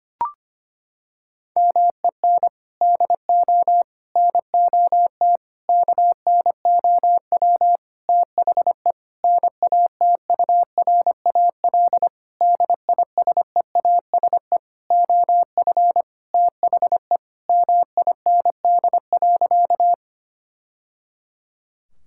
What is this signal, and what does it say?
Morse code, heard as a keyed tone, about 25 wpm, 700 Hz. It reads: MEN DO NOT KNOW THE NATURAL DISEASE OF THE MIND.
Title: Sentences from Top 1000 Words - Review Part 6 - 25wpm